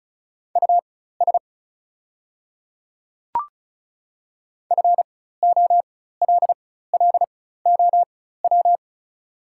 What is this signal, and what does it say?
Morse code, heard as a keyed tone, about 36 wpm, spaced as for 16 wpm, 700 Hz. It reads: US FOLLOW
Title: 4x Character Spacing - Top 200 Words - 35wpm